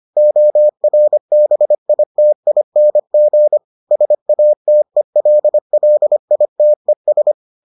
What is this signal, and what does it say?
Morse code, heard as a keyed tone, about 25 wpm, 600 Hz. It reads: ORBITING SATELLITES